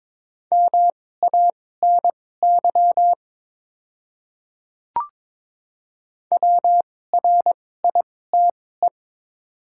Morse code, heard as a keyed tone, about 22 wpm, spaced as for 16 wpm, 700 Hz. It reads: MANY WRITE
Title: 2x Character Spacing - Top 100 Words - 22wpm